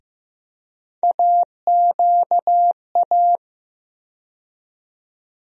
Morse code, heard as a keyed tone, about 15 wpm, 700 Hz. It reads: AQA
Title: USA Call Sign Postfixes - 3 Letters 15wpm